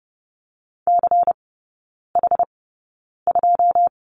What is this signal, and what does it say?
Morse code, heard as a keyed tone, about 30 wpm, spaced as for 9 wpm, 690 Hz. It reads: CH2
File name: Lesson 16e - Letter c - Three Characters 30wpm